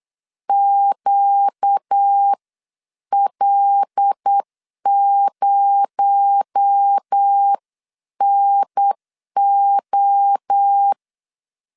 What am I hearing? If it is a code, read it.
Morse code, heard as a keyed tone, about 8 wpm, spaced as for 7 wpm, 800 Hz. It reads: QL0NO
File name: CW / DET / 1 9 1500C S A 8wpm 5637